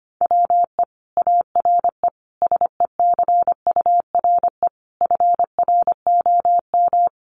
Morse code, heard as a keyed tone, about 25 wpm, 700 Hz. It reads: WE ARE SECURE FROM
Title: Sentences from Top 1200 Words 25wpm